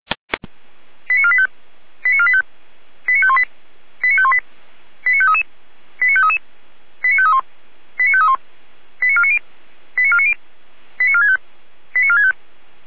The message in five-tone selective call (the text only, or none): 89485 89485 89518 89518 89533 89533 89531 89531 89490 89490 89465 89465